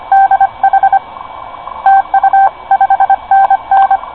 Morse code, heard as a keyed tone, about 25 wpm, 810 Hz. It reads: DH TU 5NN
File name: co8tw / cqwwcw2010